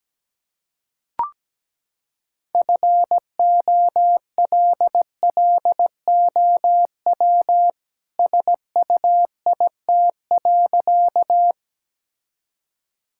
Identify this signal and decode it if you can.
Morse code, heard as a keyed tone, about 17 wpm, 700 Hz. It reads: FOLLOW SUIT.
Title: Sentences from Top 1600 Words 17wpm